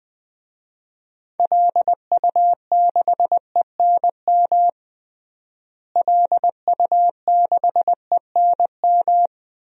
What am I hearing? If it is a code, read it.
Morse code, heard as a keyed tone, about 20 wpm, 700 Hz. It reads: LU6ENM LU6ENM
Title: DXCC-100 Call Signs - 6 Characters - Repeated 3x - 20wpm